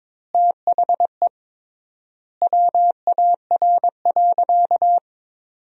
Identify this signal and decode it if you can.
Morse code, heard as a keyed tone, about 22 wpm, 700 Hz. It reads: THE WAR.
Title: 3x Word-Spacing - Sentences from Top 400 Words - 22wpm